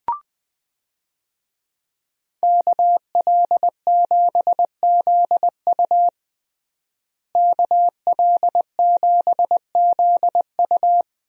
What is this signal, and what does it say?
Morse code, heard as a keyed tone, about 21 wpm, 700 Hz. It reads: KL7ZU KL7ZU
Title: USA Call Signs - 5 Characters - Repeated 2x - 20wpm